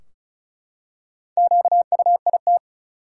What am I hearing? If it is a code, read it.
Morse code, heard as a keyed tone, about 34 wpm, 700 Hz. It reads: QUIT
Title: Four Letter Words - 35wpm